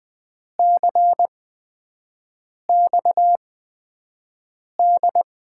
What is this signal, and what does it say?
Morse code, heard as a keyed tone, about 21 wpm, spaced as for 5 wpm, 700 Hz. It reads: CXD
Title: USA Call Signs - 6 Characters - 8x Character Spacing - 20wpm